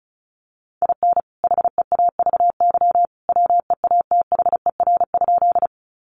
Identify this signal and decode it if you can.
Morse code, heard as a keyed tone, about 34 wpm, 700 Hz. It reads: IN HEAVY WEATHER?
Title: Sentences from Top 1100 Words 35wpm